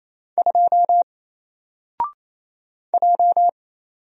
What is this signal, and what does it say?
Morse code, heard as a keyed tone, about 27 wpm, 700 Hz. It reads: 2 J
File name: Single-Letter-Number-28wpm